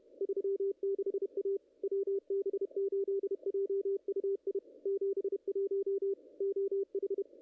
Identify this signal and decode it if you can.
Morse code, heard as a keyed tone, about 30 wpm, 410 Hz, with radio noise. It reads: 36A WB8JUI 71 OH
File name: contests / ss-cw-k5tr-op-n5rz -0000